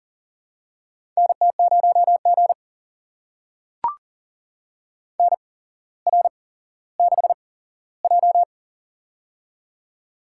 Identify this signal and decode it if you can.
Morse code, heard as a keyed tone, about 41 wpm, spaced as for 17 wpm, 700 Hz. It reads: NT0G NR6J
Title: USA Call Signs - 4 Characters - 8x Character Spacing - 40wpm